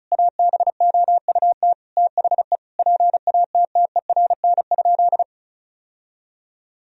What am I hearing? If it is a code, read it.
Morse code, heard as a keyed tone, about 34 wpm, 700 Hz. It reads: ABOUT THE PATTERN?
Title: Sentences from Top 1000 Words - Review Part 2 - 35wpm